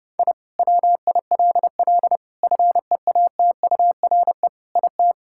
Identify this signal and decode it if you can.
Morse code, heard as a keyed tone, about 32 wpm, 700 Hz. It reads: I WILL FEATURE IT
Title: Sentences from Top 1000 Words - Review Part 5 - 30wpm